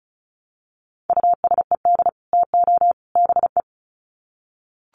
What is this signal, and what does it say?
Morse code, heard as a keyed tone, about 34 wpm, 700 Hz. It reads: USED TO BE
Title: Sets of 3 Words 35wpm